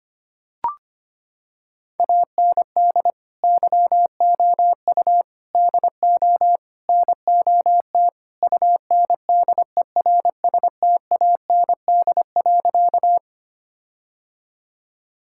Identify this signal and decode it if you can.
Morse code, heard as a keyed tone, about 25 wpm, 700 Hz. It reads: AND YOU DO NOT UNDERSTAND.
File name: practice / 25wpm / Sentences with 4-Word Phrases - 25wpm